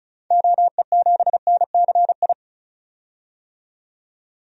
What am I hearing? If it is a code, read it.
Morse code, heard as a keyed tone, about 36 wpm, 700 Hz. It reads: OE7NCI
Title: DXCC-100 Call Signs - 6 Characters - 35wpm